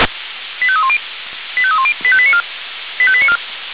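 Five-tone selective call (ZVEI), heard as quick runs of five tones, tones of about 70 ms. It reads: 85310 85310 85894 85894
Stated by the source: ZVEI five-tone selective call received by SDR.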